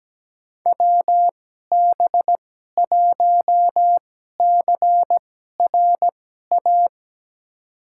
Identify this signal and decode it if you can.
Morse code, heard as a keyed tone, about 17 wpm, spaced as for 12 wpm, 700 Hz. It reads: WB1CRA
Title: USA Call Signs - 6 Characters - 2x Character Spacing - 17wpm